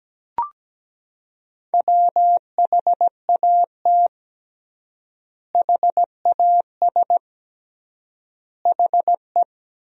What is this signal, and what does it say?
Morse code, heard as a keyed tone, about 17 wpm, 700 Hz. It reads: WHAT HAS HE
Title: 3x Word-Spacing - Sentences from Top 500 Words - 17wpm